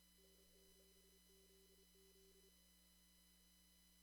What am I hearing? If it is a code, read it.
Morse code, heard as a keyed tone, about 28 wpm, 340 Hz. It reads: N7ZZ
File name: CW / kt5j wpx cw-op-k5pi